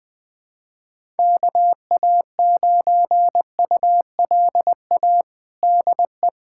Morse code, heard as a keyed tone, about 20 wpm, 700 Hz. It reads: KA9ULA DE
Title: USA Callsign DE Callsign - 6 Characters - 20wpm